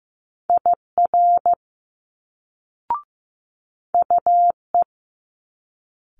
Morse code, heard as a keyed tone, about 15 wpm, 700 Hz. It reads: IR UE
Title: Multi-letter Phonemes Quick Fire 15wpm